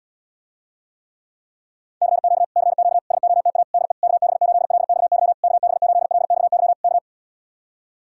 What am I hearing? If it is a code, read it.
Morse code, heard as a keyed tone, about 77 wpm, 700 Hz. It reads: CQ CQ SOTA DE KB8LXY KB8LXY K